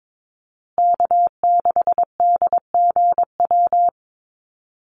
Morse code, heard as a keyed tone, about 22 wpm, 700 Hz. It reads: K6DGW
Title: Call Signs Repeated 3x - 22wpm